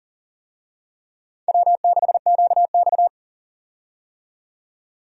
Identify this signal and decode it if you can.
Morse code, heard as a keyed tone, about 38 wpm, 690 Hz. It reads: W6QX
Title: USA Call Signs - 4 Characters - 40wpm